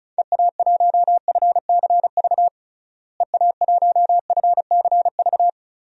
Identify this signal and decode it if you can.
Morse code, heard as a keyed tone, about 34 wpm, 700 Hz. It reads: EA1FCV EA1FCV
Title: DXCC-100 Call Signs - 6 Characters - Repeated 2x - 35wpm